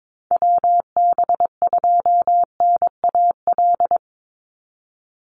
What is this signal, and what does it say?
Morse code, heard as a keyed tone, about 22 wpm, 700 Hz. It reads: WB2NAL
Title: USA Call Signs 22wpm